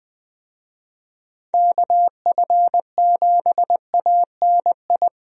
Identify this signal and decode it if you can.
Morse code, heard as a keyed tone, about 20 wpm, 700 Hz. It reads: KF7ANI
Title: USA Call Signs - 6 Characters - Repeated 2x - 20wpm